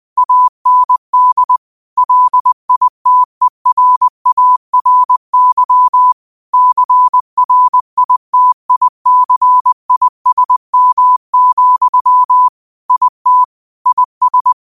Morse code, heard as a keyed tone, about 20 wpm, 1000 Hz. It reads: AND LITERARY CRITICISM, IT IS
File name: qotdfiles / 20wpm